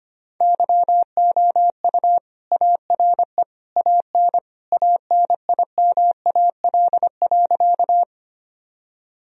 Morse code, heard as a keyed tone, about 25 wpm, 700 Hz. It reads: YOU ARE AN ANIMAL.
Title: Sentences from Top 1000 Words - Review Part 1 - 25wpm